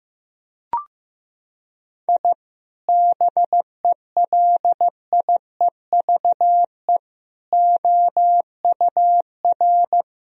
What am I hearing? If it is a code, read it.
Morse code, heard as a keyed tone, about 15 wpm, 700 Hz. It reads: I BELIEVE OUR
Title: Sentences from Top 1000 Words - Review Part 3 - 15wpm